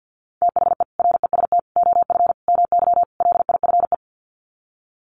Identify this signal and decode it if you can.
Morse code, heard as a keyed tone, about 50 wpm, 700 Hz. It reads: THE REST OF MY LIFE
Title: Sets of 5 Words 50wpm